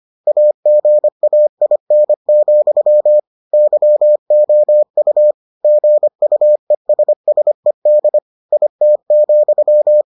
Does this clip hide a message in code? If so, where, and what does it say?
Yes: Morse code from 0.3 to 10.0 s, heard as a keyed tone, about 25 wpm, 600 Hz. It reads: AGAIN, YOU GUESSED IT,